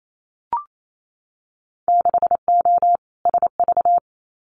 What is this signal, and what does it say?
Morse code, heard as a keyed tone, about 28 wpm, 700 Hz. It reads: 6O SV
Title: Winter Field Day Exchanges Encore 28wpm